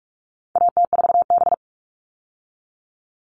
Morse code, heard as a keyed tone, about 44 wpm, 700 Hz. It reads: AT4B
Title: All Cut Numbers - 4-Digit 45wpm